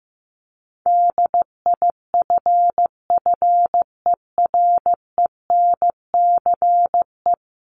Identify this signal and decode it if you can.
Morse code, heard as a keyed tone, about 15 wpm, 700 Hz. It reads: DIFFERENCE
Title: Top 751-1000 Words 15wpm